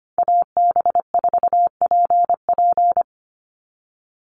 Morse code, heard as a keyed tone, about 25 wpm, 700 Hz. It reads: AB4PP
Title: NA SOTA Chaser Callsigns 25wpm